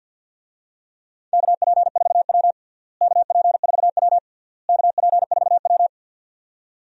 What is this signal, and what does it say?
Morse code, heard as a keyed tone, about 51 wpm, 700 Hz. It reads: KP4W KP4W KP4W
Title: DXCC-100 Call Signs - 4 Characters - Repeated 3x - 50wpm